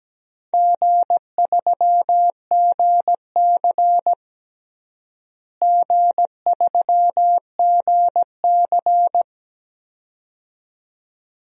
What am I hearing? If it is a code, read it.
Morse code, heard as a keyed tone, about 17 wpm, 700 Hz. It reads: G3GC G3GC
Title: DXCC-100 Call Signs - 4 Characters - Repeated 2x - 17wpm